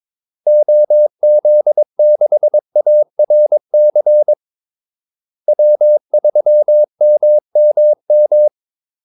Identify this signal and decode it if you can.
Morse code, heard as a keyed tone, about 22 wpm, 600 Hz. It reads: OZ6ARC W3MMM
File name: Callsigns-22wpm